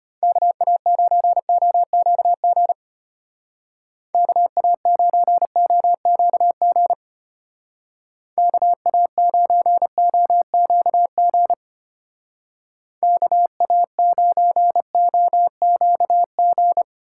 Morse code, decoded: KA9OQG KA9OQG KA9OQG KA9OQG

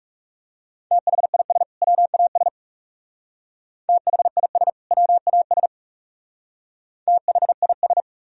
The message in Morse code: THIS WAS THIS WAS THIS

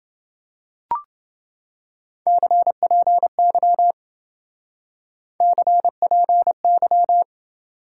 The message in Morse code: CPY CPY

